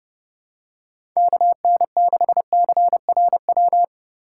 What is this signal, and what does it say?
Morse code, heard as a keyed tone, about 30 wpm, 700 Hz. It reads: KN6CRW
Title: USA Call Signs - 6 Characters - 2x Character Spacing - 30wpm